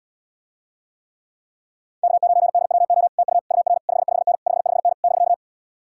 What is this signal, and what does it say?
Morse code, heard as a keyed tone, about 80 wpm, 700 Hz. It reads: K9DWO TU UR 55N 55N <BK>